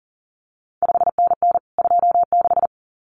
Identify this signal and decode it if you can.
Morse code, heard as a keyed tone, about 40 wpm, 700 Hz. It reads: 5NN 26
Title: CQ WW DX Contest Exchange Encore - 40wpm